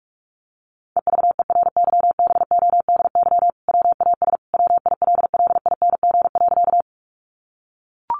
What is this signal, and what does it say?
Morse code, heard as a keyed tone, about 44 wpm, 700 Hz. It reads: EVERYBODY WAS WILLING.
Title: Sentences from Top 1400 Words 45wpm